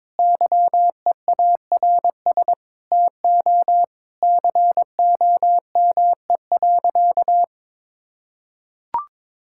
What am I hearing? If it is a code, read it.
Morse code, heard as a keyed tone, about 22 wpm, 700 Hz. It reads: YEARS TO COME.
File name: Sentences with 7-Word Phrases - 22wpm